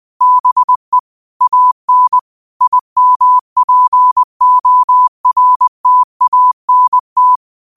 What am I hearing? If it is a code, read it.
Morse code, heard as a keyed tone, about 20 wpm, 1000 Hz. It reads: BE AN IMPORTANT
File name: qotdfiles / 20wpm